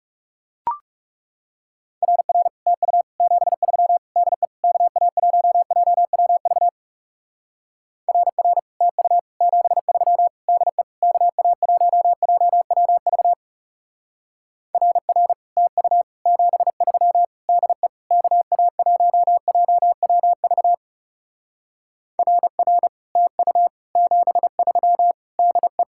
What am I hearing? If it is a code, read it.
Morse code, heard as a keyed tone, about 37 wpm, 700 Hz. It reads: RR TU 73 DE KA1JWV RR TU 73 DE KA1JWV RR TU 73 DE KA1JWV RR TU 73 DE
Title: Speed-Racing Common QSO Phrases - 30wpm